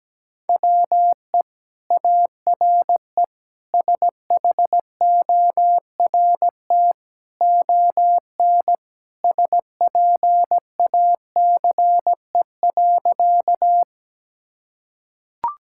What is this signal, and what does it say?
Morse code, heard as a keyed tone, about 17 wpm, 700 Hz. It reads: WE ARE SHORT ON SPACE.
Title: Sentences from Top 1000 Words - Review Part 5 - 17wpm